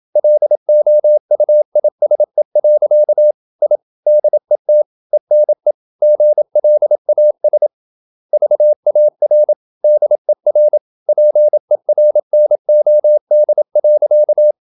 Morse code, heard as a keyed tone, about 27 wpm, 600 Hz. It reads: LOUISE. I DET ENE GLAS VAR DER PERNOD.
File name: Ferie QRQ 27wpm